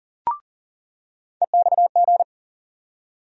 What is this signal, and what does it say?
Morse code, heard as a keyed tone, about 40 wpm, 700 Hz. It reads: EXG
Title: USA Call Sign Postfixes - 3 Letters 40wpm